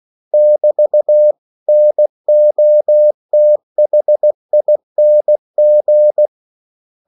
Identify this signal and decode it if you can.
Morse code, heard as a keyed tone, about 16 wpm, 600 Hz. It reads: <BT>NOTHING
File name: CW / Practice / 500MostCommonEnglishWords16